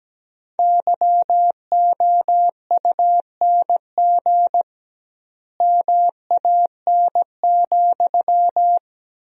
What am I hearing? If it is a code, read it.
Morse code, heard as a keyed tone, about 17 wpm, 700 Hz. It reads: YOUNG MAN,
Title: 2x Word-Spacing - Sentences from Top 400 Words - 17wpm